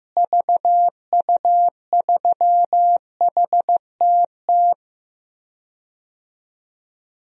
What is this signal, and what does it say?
Morse code, heard as a keyed tone, about 15 wpm, 700 Hz. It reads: VU3HTT
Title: DXCC-100 Call Signs - 6 Characters - Repeated 2x - 15wpm